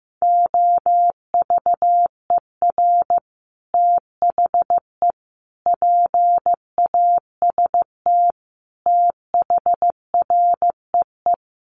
Morse code, heard as a keyed tone, about 15 wpm, 700 Hz. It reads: OVER THE PAST THREE
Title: Sets of 5 Words Encore2 15wpm